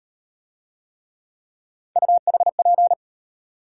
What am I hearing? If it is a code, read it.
Morse code, heard as a keyed tone, about 36 wpm, 700 Hz. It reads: UHP